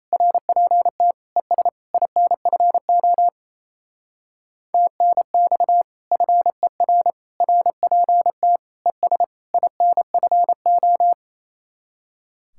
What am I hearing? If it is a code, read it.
Morse code, heard as a keyed tone, about 36 wpm, 700 Hz. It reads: RPT ES INFO TNX FER RPT ES INFO